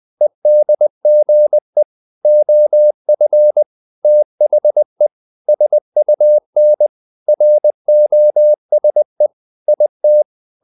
Morse code, heard as a keyed tone, about 20 wpm, 600 Hz. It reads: EDGE OF THE SUN ROSE IT
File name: the old man and the sea / the old man and the sea part 1 20wpm